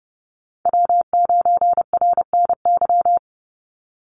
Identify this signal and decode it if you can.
Morse code, heard as a keyed tone, about 30 wpm, 700 Hz. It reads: W9RNY
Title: Call Signs - 30wpm